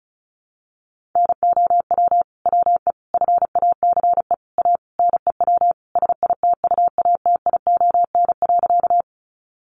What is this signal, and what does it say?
Morse code, heard as a keyed tone, about 34 wpm, 700 Hz. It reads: NOW WE FACE A NEW SITUATION.
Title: Sentences from Top 1200 Words 35wpm